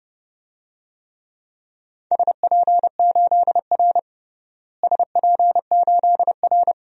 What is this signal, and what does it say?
Morse code, heard as a keyed tone, about 30 wpm, 700 Hz. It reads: SP8R SP8R
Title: DXCC-100 Call Signs - 4 Characters - Repeated 3x - 30wpm